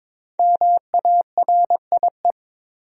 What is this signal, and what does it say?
Morse code, heard as a keyed tone, about 22 wpm, 700 Hz. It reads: MARIE
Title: USA Common Names - 22wpm